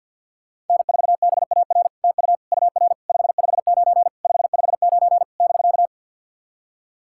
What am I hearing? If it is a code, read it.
Morse code, heard as a keyed tone, about 50 wpm, 700 Hz. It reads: N4BAR TU UR 559 559 <BK>
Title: SOTA QSO Practice 50wpm